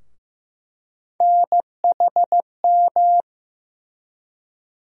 Morse code, heard as a keyed tone, about 15 wpm, 700 Hz. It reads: NHM
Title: USA Call Sign Postfixes - 3 Letters 15wpm